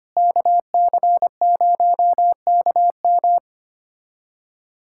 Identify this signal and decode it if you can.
Morse code, heard as a keyed tone, about 25 wpm, 700 Hz. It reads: KC0KM
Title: USA Callsign DE Callsign - 5 Characters - 25wpm